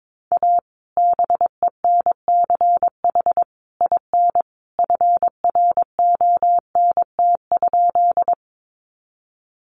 Morse code, heard as a keyed tone, about 22 wpm, 700 Hz. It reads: A BENCH IN FRONT?
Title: Sentences from Top 1400 Words 22wpm